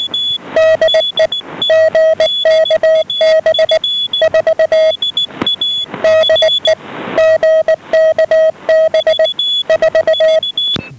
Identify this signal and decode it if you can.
Morse code, heard as a keyed tone, about 19 wpm, 640 Hz, with noise. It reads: DE GKB 4 DE GKB 4
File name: GKB4 12835.4 1130